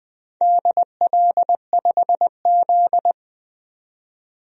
Morse code, heard as a keyed tone, about 20 wpm, 700 Hz. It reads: DL5Z
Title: DXCC-100 Call Signs - 4 Characters - 20wpm